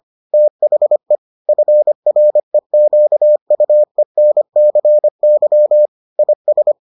Morse code, heard as a keyed tone, about 25 wpm, 600 Hz. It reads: THE FREQUENCY IS